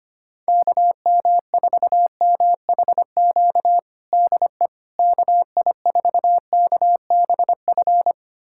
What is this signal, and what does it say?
Morse code, heard as a keyed tone, about 25 wpm, 700 Hz. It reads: KM4MHQ DE KI4KBF
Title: USA Callsign DE Callsign - 6 Characters - 25wpm